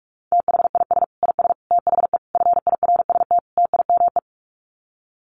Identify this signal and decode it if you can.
Morse code, heard as a keyed tone, about 44 wpm, 700 Hz. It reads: THIS IS THE FIRST TIME
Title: Sets of 5 Words 45wpm